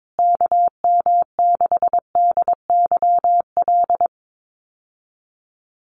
Morse code, heard as a keyed tone, about 22 wpm, 700 Hz. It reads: KM6DYL
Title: USA Call Signs Repeated 3x 22wpm